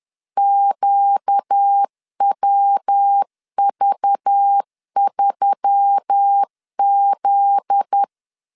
Morse code, heard as a keyed tone, about 10 wpm, 800 Hz. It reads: QWV3Z
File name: LEZIONI / 7 GF9Z / 7 GF9Z 10wpm 2611